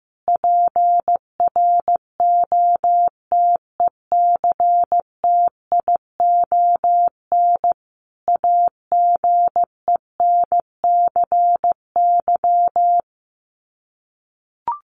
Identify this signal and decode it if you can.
Morse code, heard as a keyed tone, about 15 wpm, 700 Hz. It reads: PROTECTION AGENCY
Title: Sets of 5 Words Encore2 15wpm